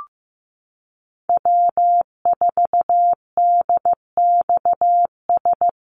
Morse code, heard as a keyed tone, about 15 wpm, 700 Hz. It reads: W4DXS
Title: USA Call Signs Repeated 2x 15wpm